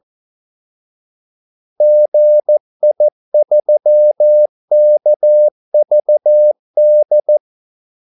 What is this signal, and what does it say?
Morse code, heard as a keyed tone, about 14 wpm, 600 Hz. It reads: GI3KVD
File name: Callsigns-14wpm